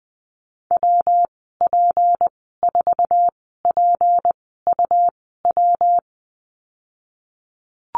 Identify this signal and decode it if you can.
Morse code, heard as a keyed tone, about 20 wpm, spaced as for 15 wpm, 700 Hz. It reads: WP4PUW
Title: 173 Lesson 28g - Number 9 - Call Signs 20wpm